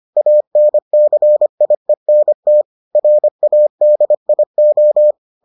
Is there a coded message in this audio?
Yes: Morse code from 0.2 to 5.1 s, heard as a keyed tone, about 25 wpm, 600 Hz. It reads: ANCIENT RADIO